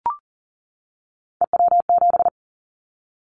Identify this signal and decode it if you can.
Morse code, heard as a keyed tone, about 39 wpm, 700 Hz. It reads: EW7
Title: Callsign Prefixes for Top 50 DXCC - 40wpm